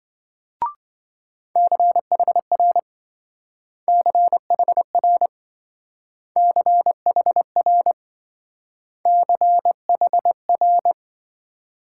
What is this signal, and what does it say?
Morse code, heard as a keyed tone, about 25 wpm, 700 Hz. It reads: CHR CHR CHR CHR